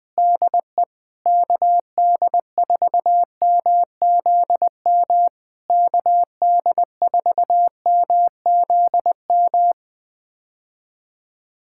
Morse code, heard as a keyed tone, about 20 wpm, 700 Hz. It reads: DE KD4MZM KD4MZM
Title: POTA QSO Practice - 20wpm